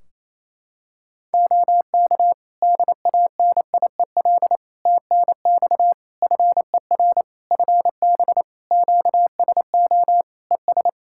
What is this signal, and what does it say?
Morse code, heard as a keyed tone, about 28 wpm, 700 Hz. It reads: OK DANIEL TNX FER FB QSO ES
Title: Practice QSO - Exchange 4 - Ending - 28wpm